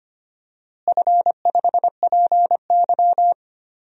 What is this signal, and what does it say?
Morse code, heard as a keyed tone, about 25 wpm, 700 Hz. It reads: F5PY
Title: DXCC-100 Call Signs - 4 Characters - Repeated 3x - 25wpm